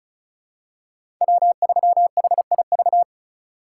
Morse code, heard as a keyed tone, about 34 wpm, 700 Hz. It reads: W3HIV
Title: USA Call Signs - 5 Characters - 3x Character Spacing - 35wpm